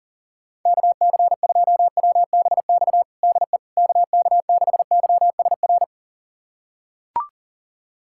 Morse code, heard as a keyed tone, about 40 wpm, 700 Hz. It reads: KC2WBX DE KK6YSR
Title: USA Callsign DE Callsign - 6 Characters - 40wpm